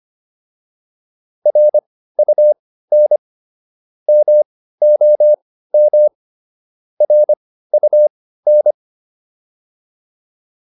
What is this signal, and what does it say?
Morse code, heard as a keyed tone, about 25 wpm, spaced as for 15 wpm, 600 Hz. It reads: RUN MOM RUN